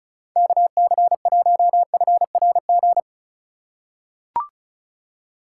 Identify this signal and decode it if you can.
Morse code, heard as a keyed tone, about 36 wpm, 700 Hz. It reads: KC1FRG
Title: USA Call Signs - 6 Characters - 3x Character Spacing - 35wpm